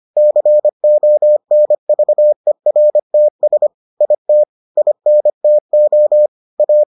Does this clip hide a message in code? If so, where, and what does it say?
Yes: Morse code from 0.2 to 6.8 s, heard as a keyed tone, about 25 wpm, 600 Hz. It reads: CONVERTS IT INTO A